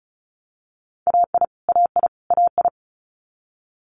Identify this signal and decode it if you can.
Morse code, heard as a keyed tone, about 36 wpm, 700 Hz. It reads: AI AI AI
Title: Multi-letter Phonemes Repeated 3x 35wpm